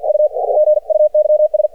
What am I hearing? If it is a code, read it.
Morse code, heard as a keyed tone, about 55 wpm, spaced as for 48 wpm, 610 Hz. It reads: U GUYS
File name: archived / CRSnet